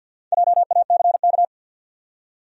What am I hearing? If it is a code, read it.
Morse code, heard as a keyed tone, about 52 wpm, 700 Hz. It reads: JACK
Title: USA Common Names - Max 4 Letters - 50wpm